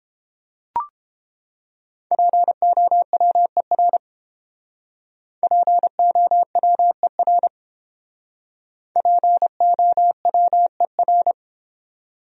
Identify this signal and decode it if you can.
Morse code, heard as a keyed tone, about 31 wpm, 700 Hz. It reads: POWER POWER POWER